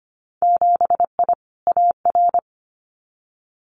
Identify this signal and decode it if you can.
Morse code, heard as a keyed tone, about 25 wpm, 700 Hz. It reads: 7I AR
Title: Winter Field Day Exchanges Encore 25wpm